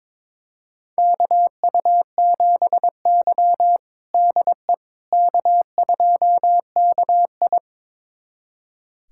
Morse code, heard as a keyed tone, about 21 wpm, 700 Hz. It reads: KU7Y DE K2KI
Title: USA Callsign DE Callsign - 4 Characters - 22wpm